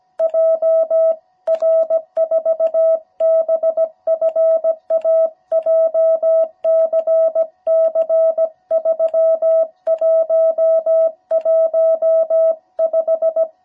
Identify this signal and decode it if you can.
Morse code, heard as a keyed tone, about 17 wpm, 660 Hz. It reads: JR4BFA JCC3115
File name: jh8cbh / 2021cw